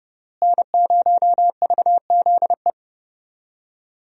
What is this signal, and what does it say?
Morse code, heard as a keyed tone, about 30 wpm, 700 Hz. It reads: N0VZE
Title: USA Call Signs - 5 Characters - Repeated 3x - 30wpm